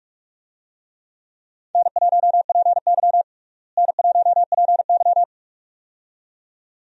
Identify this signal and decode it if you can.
Morse code, heard as a keyed tone, about 46 wpm, 700 Hz. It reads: N1PY N1PY
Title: USA Call Signs - 4 Characters - Repeated 2x - 45wpm